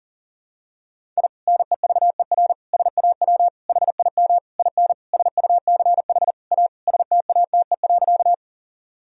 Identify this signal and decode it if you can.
Morse code, heard as a keyed tone, about 39 wpm, 700 Hz. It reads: I NEVER SAW HIM IN SUCH A STATE.
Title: Sentences from Top 1000 Words - Review Part 2 - 40wpm